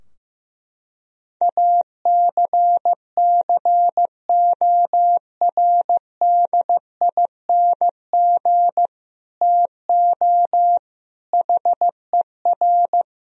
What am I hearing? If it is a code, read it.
Morse code, heard as a keyed tone, about 15 wpm, 700 Hz. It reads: ACCORDING TO HER
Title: Sentences from Top 1000 Words - Review Part 4 - 15wpm